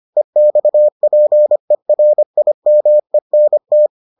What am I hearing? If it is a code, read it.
Morse code, heard as a keyed tone, about 25 wpm, 600 Hz. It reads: EXPERIMENT